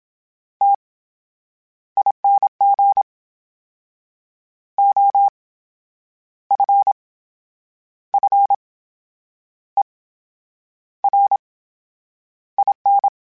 Morse code, heard as a keyed tone, about 27 wpm, spaced as for 9 wpm, 800 Hz. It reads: TING OFFERIN
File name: Special INGs / Special INGs-mp3s / VerySlowToFaster Ing @25